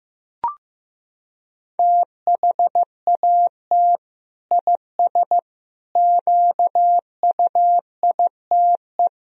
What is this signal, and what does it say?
Morse code, heard as a keyed tone, about 15 wpm, 700 Hz. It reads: THAT IS QUITE
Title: Sentences from Top 1000 Words - Review Part 2 - 15wpm